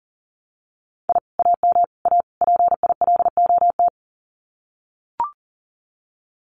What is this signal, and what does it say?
Morse code, heard as a keyed tone, about 40 wpm, 700 Hz. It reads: I AM A PILOT
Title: Common QSO Phrases Encore 40wpm